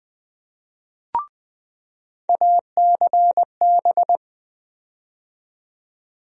Morse code, heard as a keyed tone, about 20 wpm, 700 Hz. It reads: ACB
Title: USA Call Sign Postfixes - 3 Letters 20wpm